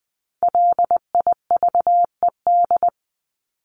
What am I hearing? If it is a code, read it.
Morse code, heard as a keyed tone, about 20 wpm, 700 Hz. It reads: LIVED
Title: ED Suffix - Top 100 Words with Base Words 20wpm